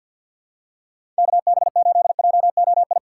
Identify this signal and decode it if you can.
Morse code, heard as a keyed tone, about 46 wpm, 700 Hz. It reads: KB8JOI
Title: USA Call Signs - 6 Characters - 3x Character Spacing - 50wpm